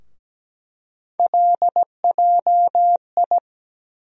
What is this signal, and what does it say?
Morse code, heard as a keyed tone, about 17 wpm, 700 Hz. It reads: LJI
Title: USA Call Sign Postfixes - 3 Letters 17wpm